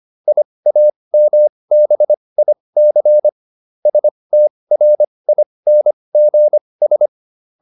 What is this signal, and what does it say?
Morse code, heard as a keyed tone, about 25 wpm, spaced as for 20 wpm, 600 Hz. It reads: IAMBIC STRINGS
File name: SS315 20